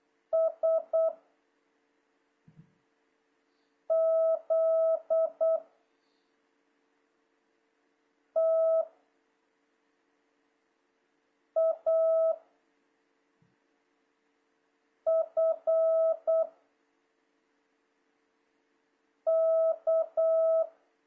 Morse code, heard as a keyed tone, about 8 wpm, 650 Hz. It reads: S Z T A F K
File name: jh8cbh / 2026cw